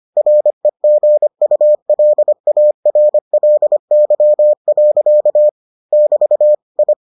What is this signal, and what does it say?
Morse code, heard as a keyed tone, about 25 wpm, 600 Hz. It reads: REGULARLY. <BT> I